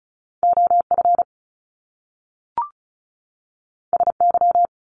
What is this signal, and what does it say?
Morse code, heard as a keyed tone, about 36 wpm, 700 Hz. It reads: OF SY
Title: Top Two Letter Combinations in Words 35wpm